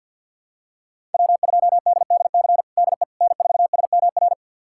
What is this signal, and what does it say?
Morse code, heard as a keyed tone, about 47 wpm, 700 Hz. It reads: W2DDC DE N4SMR
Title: USA Callsign DE Callsign - 5 Characters - 50wpm